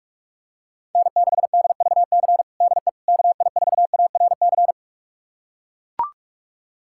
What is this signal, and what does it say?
Morse code, heard as a keyed tone, about 41 wpm, 700 Hz. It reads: N6DVC DE KI4ARC
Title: USA Callsign DE Callsign 45wpm